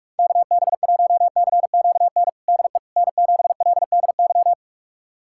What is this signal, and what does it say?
Morse code, heard as a keyed tone, about 41 wpm, 700 Hz. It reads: KB1CQN DE N7LDY
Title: USA Callsign DE Callsign 45wpm